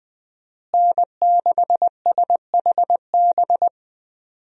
Morse code, heard as a keyed tone, about 20 wpm, 700 Hz. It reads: N6SHB
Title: USA Call Signs - 5 Characters - Repeated 2x - 20wpm